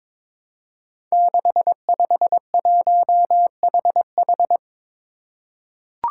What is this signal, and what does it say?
Morse code, heard as a keyed tone, about 22 wpm, 700 Hz. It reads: 651HH
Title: Five Challenging Letter-Number Combinations 22wpm